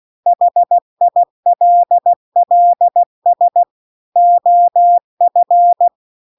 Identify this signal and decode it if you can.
Morse code, heard as a keyed tone, about 16 wpm, 700 Hz. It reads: HILLS OF
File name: CW-The-Big-Twoheaded-River-1-16WPM